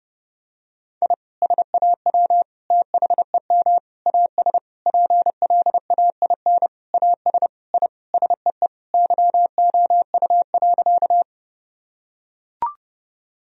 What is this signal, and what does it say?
Morse code, heard as a keyed tone, about 30 wpm, 700 Hz. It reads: I SAW THEM AS PLAIN AS I SEE YOU.
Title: Sentences from Top 1000 Words - Review Part 2 - 30wpm